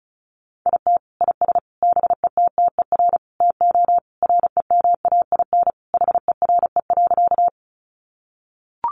Morse code, heard as a keyed tone, about 34 wpm, 700 Hz. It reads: IT IS BETTER TO REMAIN HERE.
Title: Sentences from Top 1200 Words 35wpm